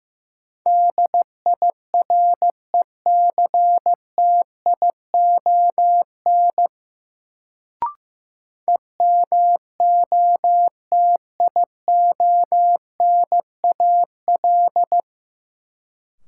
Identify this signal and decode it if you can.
Morse code, heard as a keyed tone, about 16 wpm, 700 Hz. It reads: DIRECTION EMOTIONAL
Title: Sentences from Top 1000 Words - Review Part 5 - New Word Drill - 15wpm